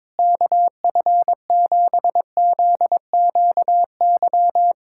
Morse code, heard as a keyed tone, about 21 wpm, 700 Hz. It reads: KF7ZQY
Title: USA Callsign DE Callsign - 6 Characters - 22wpm